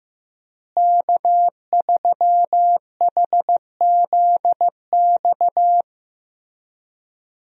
Morse code, heard as a keyed tone, about 15 wpm, 700 Hz. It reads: K3HZX
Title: USA Call Signs - 5 Characters - Repeated 2x - 15wpm